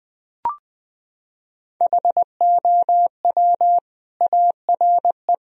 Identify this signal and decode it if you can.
Morse code, heard as a keyed tone, about 20 wpm, 700 Hz. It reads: HOW ARE
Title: Sentences from Top 1000 Words - Review Part 4 - 20wpm